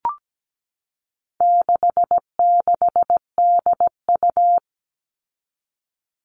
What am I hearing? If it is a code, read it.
Morse code, heard as a keyed tone, about 17 wpm, 700 Hz. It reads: 66DU
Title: All Cut Numbers - 4-Digit 17wpm